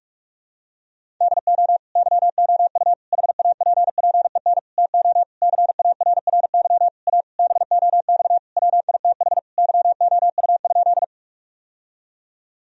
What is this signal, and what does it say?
Morse code, heard as a keyed tone, about 43 wpm, 700 Hz. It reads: DO YOU HAPPEN TO CARRY A BOX WITH YOU?
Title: Sentences from Top 1000 Words - Review Part 2 - 45wpm